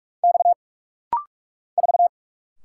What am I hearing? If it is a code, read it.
Morse code, heard as a keyed tone, about 46 wpm, 700 Hz. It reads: X 4
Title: Single Letter-Number Alt - Rapid-Fire 45wpm